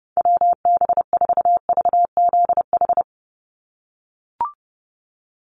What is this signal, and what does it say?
Morse code, heard as a keyed tone, about 30 wpm, 700 Hz. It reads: WB4VZH
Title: USA Call Signs 30wpm